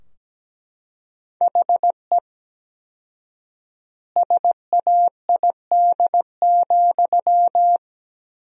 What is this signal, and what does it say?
Morse code, heard as a keyed tone, about 17 wpm, 700 Hz. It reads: HE SAID,
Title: 4x Word-Spacing - Sentences from Top 400 Words - 17wpm